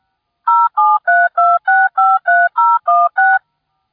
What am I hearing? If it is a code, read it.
DTMF, heard as telephone touch tones, tones of about 220 ms. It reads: 0732653016